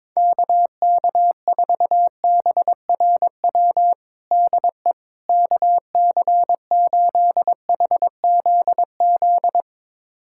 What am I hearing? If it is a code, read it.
Morse code, heard as a keyed tone, about 22 wpm, 700 Hz. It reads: KK4BRW DE KC8HZZ
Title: USA Callsign DE Callsign - 6 Characters - 22wpm